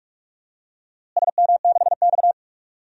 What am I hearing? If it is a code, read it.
Morse code, heard as a keyed tone, about 42 wpm, 700 Hz. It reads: SM6X
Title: DXCC-100 Call Signs - 4 Characters - Repeated 2x - 45wpm